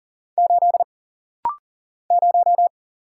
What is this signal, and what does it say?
Morse code, heard as a keyed tone, about 40 wpm, 700 Hz. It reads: 8 0
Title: Single Letter-Number - Rapid-Fire 40wpm